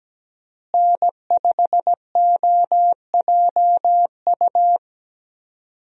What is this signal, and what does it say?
Morse code, heard as a keyed tone, about 17 wpm, 700 Hz. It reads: N5OJU
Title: USA Call Signs - 5 Characters - Repeated 2x - 17wpm